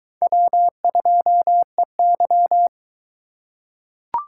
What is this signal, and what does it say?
Morse code, heard as a keyed tone, about 21 wpm, 700 Hz. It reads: W2EY